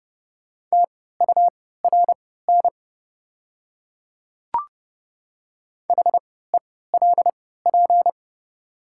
Morse code, heard as a keyed tone, about 30 wpm, spaced as for 17 wpm, 690 Hz. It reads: TURN HELP
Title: 3x Character Spacing - Top 200 Words - 30wpm